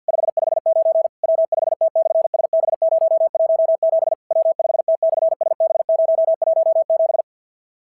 Morse code, heard as a keyed tone, about 51 wpm, 650 Hz. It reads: 559 W5T/SB017 W5T/SB017
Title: SOTA QSO Practice 50wpm